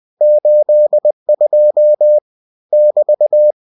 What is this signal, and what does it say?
Morse code, heard as a keyed tone, about 20 wpm, 600 Hz. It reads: 82 <BT>